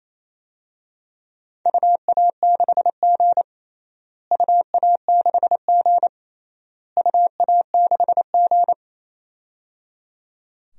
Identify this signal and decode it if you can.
Morse code, heard as a keyed tone, about 28 wpm, 700 Hz. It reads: UA6G UA6G UA6G
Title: DXCC-100 Call Signs - 4 Characters - Repeated 3x - 28wpm